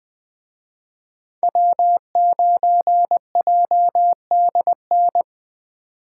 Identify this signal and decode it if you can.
Morse code, heard as a keyed tone, about 20 wpm, 700 Hz. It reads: W9JDN